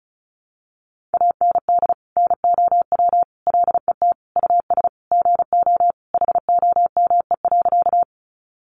Morse code, heard as a keyed tone, about 34 wpm, 700 Hz. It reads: AND NOW LET US GO HOME.
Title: Sentences from Top 300 Words 35wpm